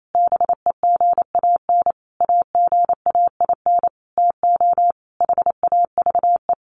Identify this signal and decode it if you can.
Morse code, heard as a keyed tone, about 28 wpm, 700 Hz. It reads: BEGAN AGAIN TO HAVE
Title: Sentences from Top 1500 Words 28wpm